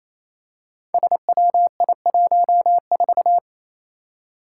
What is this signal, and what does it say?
Morse code, heard as a keyed tone, about 29 wpm, 700 Hz. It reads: SWI14
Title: Five Challenging Letter-Number Combinations 28wpm